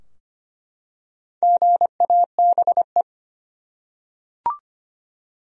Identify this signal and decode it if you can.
Morse code, heard as a keyed tone, about 25 wpm, 700 Hz. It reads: GABE
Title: USA Common Names - 25wpm